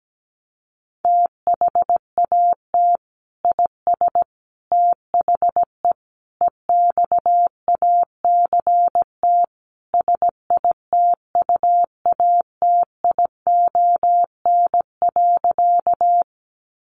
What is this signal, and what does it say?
Morse code, heard as a keyed tone, about 17 wpm, 700 Hz. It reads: THAT IS THE EXACT SITUATION.
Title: Sentences from Top 1300 Words 17wpm